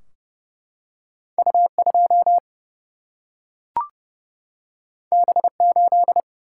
Morse code, heard as a keyed tone, about 30 wpm, 700 Hz. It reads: U2 B8
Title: Single-Letter Single-Number 30wpm